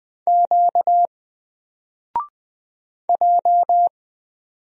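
Morse code, heard as a keyed tone, about 20 wpm, 700 Hz. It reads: Q J
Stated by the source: Single letter and number Morse drill at 20 wpm, from N4LCD.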